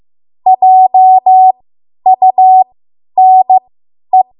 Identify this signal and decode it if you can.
Morse code, heard as a keyed tone, about 15 wpm, spaced as for 10 wpm, 750 Hz. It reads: JUNE
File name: Morse / 10WPM